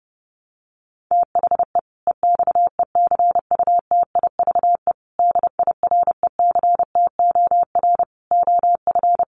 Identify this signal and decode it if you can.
Morse code, heard as a keyed tone, about 28 wpm, 700 Hz. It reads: THE EXECUTIVE DIRECTOR OF
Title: Sets of 5 Words Encore 30wpm